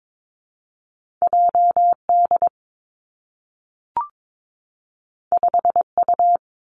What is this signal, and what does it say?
Morse code, heard as a keyed tone, about 22 wpm, 700 Hz. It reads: J D 5 U
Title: Two Challenging Letter-Number Combinations 22wpm